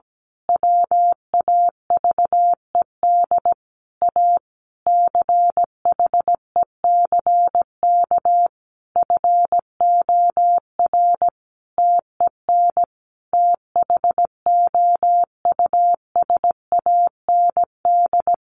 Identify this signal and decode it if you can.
Morse code, heard as a keyed tone, about 17 wpm, 700 Hz. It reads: WAVED A CHECK FOR TEN THOUSAND
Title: A Sound Of Thunder 17wpm